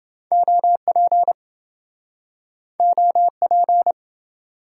Morse code, heard as a keyed tone, about 26 wpm, 700 Hz. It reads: OP OP